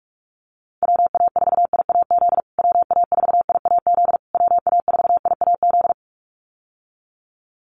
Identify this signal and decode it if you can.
Morse code, heard as a keyed tone, about 44 wpm, 700 Hz. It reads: WA4IAZ WA4IAZ WA4IAZ
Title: USA Call Signs Repeated 3x 45wpm